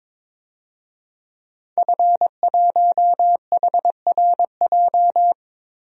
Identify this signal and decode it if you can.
Morse code, heard as a keyed tone, about 22 wpm, 700 Hz. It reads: F1HRJ
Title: DXCC-100 Call Signs - 5 Characters - 22wpm